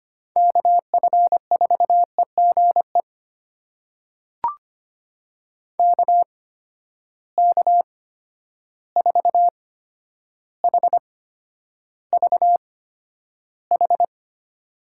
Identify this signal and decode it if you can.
Morse code, heard as a keyed tone, about 25 wpm, spaced as for 11 wpm, 700 Hz. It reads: KF4EGE KK4HVH
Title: USA Call Signs - 6 Characters - 8x Character Spacing - 25wpm